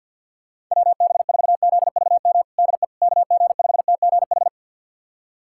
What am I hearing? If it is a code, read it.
Morse code, heard as a keyed tone, about 51 wpm, 700 Hz. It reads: WB4ZVM DE KG5TGH
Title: USA Callsign DE Callsign - 6 Characters - 50wpm